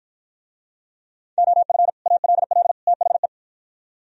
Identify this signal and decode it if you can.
Morse code, heard as a keyed tone, about 52 wpm, 700 Hz. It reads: OF ALL THE